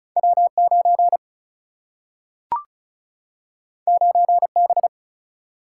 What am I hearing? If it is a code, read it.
Morse code, heard as a keyed tone, about 36 wpm, 700 Hz. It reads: W 9 9 B
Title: Single-Letter Single-Number 35wpm